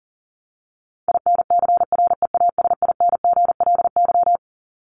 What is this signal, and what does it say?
Morse code, heard as a keyed tone, about 40 wpm, 700 Hz. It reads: INCREASINGLY
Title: LY Suffix - Top 100 Words with Base Words 40wpm